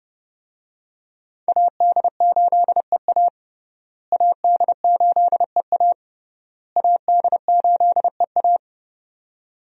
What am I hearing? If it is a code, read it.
Morse code, heard as a keyed tone, about 30 wpm, 700 Hz. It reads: AD8EA AD8EA AD8EA
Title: USA Call Signs - 5 Characters - Repeated 3x - 30wpm